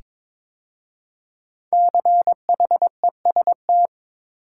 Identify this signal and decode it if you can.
Morse code, heard as a keyed tone, about 22 wpm, 700 Hz. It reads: CHEST
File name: 5 Letter Words - First 3 Letters Given - 22wpm